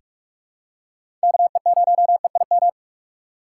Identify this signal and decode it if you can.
Morse code, heard as a keyed tone, about 46 wpm, 700 Hz. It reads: KE0EIM
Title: USA Call Signs - 6 Characters - 3x Character Spacing - 45wpm